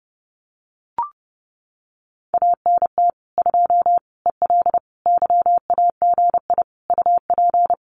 Morse code, heard as a keyed tone, about 30 wpm, 700 Hz. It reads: ANT 2 EL YAGI UP
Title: Common QSO Phrases Encore 30wpm